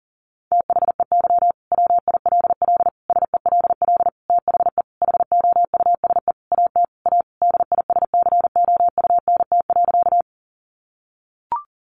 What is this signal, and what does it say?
Morse code, heard as a keyed tone, about 40 wpm, 700 Hz. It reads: THEY WILL SELL THE HOUSE AT A DISCOUNT.
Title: Sentences from Top 1200 Words 40wpm